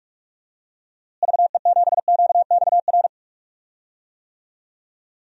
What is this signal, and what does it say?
Morse code, heard as a keyed tone, about 46 wpm, 700 Hz. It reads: VE7QXR
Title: DXCC-100 Call Signs - 6 Characters - Repeated 2x - 45wpm